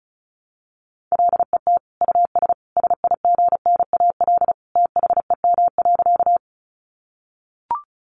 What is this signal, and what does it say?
Morse code, heard as a keyed tone, about 34 wpm, 700 Hz. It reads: LET US SIGNAL THEM.
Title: Sentences from Top 1400 Words 35wpm